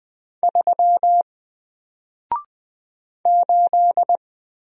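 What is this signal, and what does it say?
Morse code, heard as a keyed tone, about 20 wpm, 700 Hz. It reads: 3 8
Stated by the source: Single letter and number Morse drill at 20 wpm, from N4LCD.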